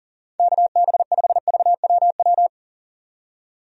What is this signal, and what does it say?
Morse code, heard as a keyed tone, about 40 wpm, 700 Hz. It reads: KB5VWW
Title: USA Call Signs - 6 Characters - 40wpm